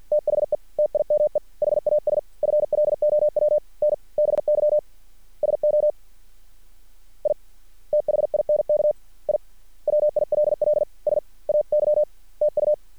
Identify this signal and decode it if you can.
Morse code, heard as a keyed tone, about 50 wpm, spaced as for 44 wpm, 600 Hz. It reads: THE TIME HAS FLOWN BY SO I THINK I WILL SAY TU
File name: archived / CRSnet